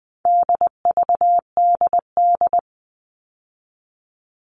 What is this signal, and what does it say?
Morse code, heard as a keyed tone, about 20 wpm, 700 Hz. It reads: DVDD
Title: All Cut Numbers - 4-Digit 20wpm